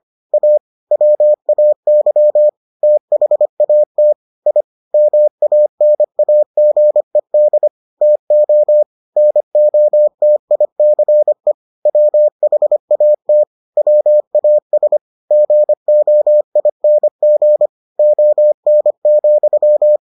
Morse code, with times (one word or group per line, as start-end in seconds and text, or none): 0.3-0.6 A
0.9-2.5 WAY
2.8-4.1 THAT
4.5-4.6 I
4.9-7.7 MANAGED
8.0-8.8 TO
9.2-11.5 NOTICE
11.9-13.4 WHAT
13.8-15.0 WAS
15.3-17.7 GOING
18.0-20.0 ON,